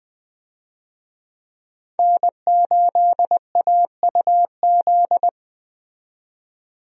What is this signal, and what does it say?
Morse code, heard as a keyed tone, about 20 wpm, 700 Hz. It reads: N8AUZ
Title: USA Call Signs - 5 Characters - 20wpm